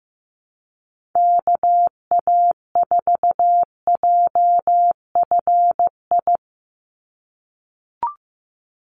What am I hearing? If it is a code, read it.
Morse code, heard as a keyed tone, about 15 wpm, 700 Hz. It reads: KA4JFI
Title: USA Call Signs 15wpm